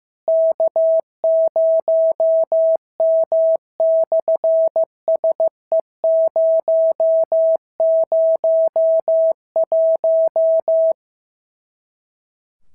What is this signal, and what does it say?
Morse code, heard as a keyed tone, about 15 wpm, 650 Hz. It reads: K0M/SE001
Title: SOTA QSO Practice 15wpm